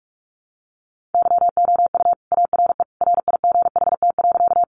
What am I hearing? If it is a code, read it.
Morse code, heard as a keyed tone, about 44 wpm, 700 Hz. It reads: YOU ARE RIGHT.
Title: Sentences from Top 200 Words 45wpm